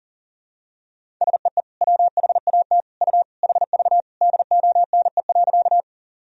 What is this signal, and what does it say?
Morse code, heard as a keyed tone, about 40 wpm, 700 Hz. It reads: SEE WHAT U HV DONE.
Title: Sentences from Top 500 Words and Abbreviations 40wpm